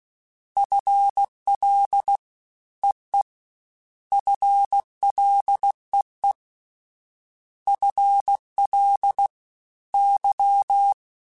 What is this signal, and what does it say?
Morse code, heard as a keyed tone, about 16 wpm, 800 Hz. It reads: FL EE FLEE FL Y
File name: FL / FL mp3s / Fast FL @15